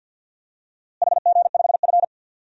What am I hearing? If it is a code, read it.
Morse code, heard as a keyed tone, about 52 wpm, 700 Hz. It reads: HG5F
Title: DXCC-100 Call Signs - 4 Characters - Repeated 3x - 50wpm